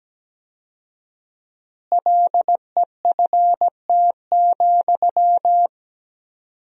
Morse code, heard as a keyed tone, about 17 wpm, 700 Hz. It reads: LEFT,
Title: 4x Word-Spacing - Sentences from Top 400 Words - 17wpm